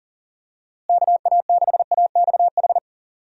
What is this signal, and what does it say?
Morse code, heard as a keyed tone, about 40 wpm, 700 Hz. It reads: KA6AXH
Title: USA Call Signs - 6 Characters - Repeated 3x - 40wpm